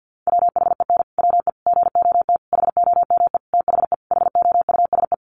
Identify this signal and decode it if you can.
Morse code, heard as a keyed tone, about 50 wpm, 700 Hz. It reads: WHEN WE GOT HOME THE HOUSE
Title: Sentences from Top 500 Words 50wpm